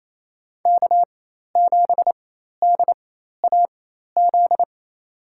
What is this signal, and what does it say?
Morse code, heard as a keyed tone, about 28 wpm, spaced as for 13 wpm, 700 Hz. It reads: K7DAZ
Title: USA Call Signs - 5 Characters - 4x Character Spacing - 28wpm